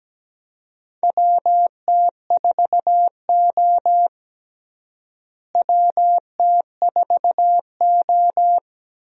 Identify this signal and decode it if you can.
Morse code, heard as a keyed tone, about 17 wpm, 700 Hz. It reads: WT4O WT4O
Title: USA Call Signs - 4 Characters - Repeated 2x - 17wpm